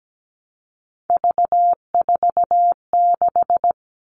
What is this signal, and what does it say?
Morse code, heard as a keyed tone, about 17 wpm, 700 Hz. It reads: V46
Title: All Cut Numbers - 3-Digit 17wpm